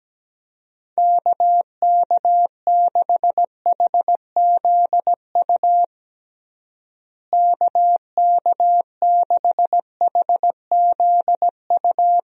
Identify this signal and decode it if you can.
Morse code, heard as a keyed tone, about 17 wpm, 700 Hz. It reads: KK6HZU KK6HZU
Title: USA Call Signs - 6 Characters - Repeated 3x - 17wpm